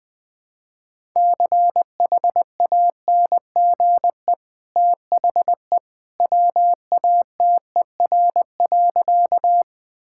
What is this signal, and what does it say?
Morse code, heard as a keyed tone, about 20 wpm, 700 Hz. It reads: CHANGE THE WATER.
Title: Easy Sentences with Hints - 20wpm